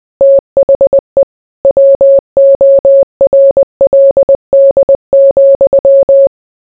Morse code, heard as a keyed tone, about 20 wpm, 550 Hz. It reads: THE WORLD,